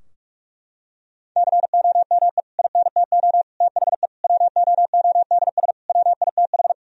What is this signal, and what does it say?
Morse code, heard as a keyed tone, about 46 wpm, 700 Hz. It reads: COME INTO THE WOODS WITH
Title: Sentences from Top 1000 Words - Review Part 2 - 45wpm